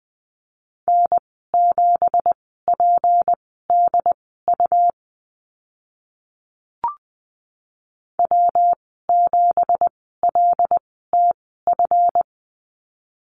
Morse code, heard as a keyed tone, about 20 wpm, spaced as for 15 wpm, 700 Hz. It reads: N7PDU W7LTF
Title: 173 Lesson 28g - Number 9 - Call Signs 20wpm